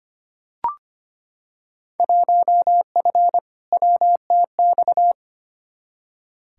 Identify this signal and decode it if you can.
Morse code, heard as a keyed tone, about 25 wpm, 700 Hz. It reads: 1F WTX
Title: Field Day Exchanges - 25wpm